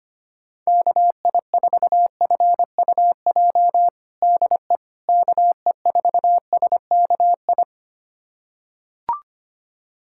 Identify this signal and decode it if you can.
Morse code, heard as a keyed tone, about 25 wpm, 700 Hz. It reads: KI4FUJ DE KE4SKI
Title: USA Callsign DE Callsign - 6 Characters - 25wpm